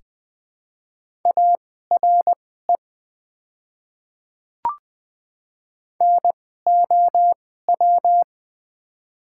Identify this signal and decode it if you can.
Morse code, heard as a keyed tone, about 20 wpm, spaced as for 14 wpm, 700 Hz. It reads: ARE NOW
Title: 2x Character Spacing - Top 100 Words - 20wpm